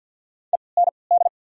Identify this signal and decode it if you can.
Morse code, heard as a keyed tone, about 54 wpm, spaced as for 29 wpm, 700 Hz. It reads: END
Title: 3x Character Spacing - Top 300 Words - 50wpm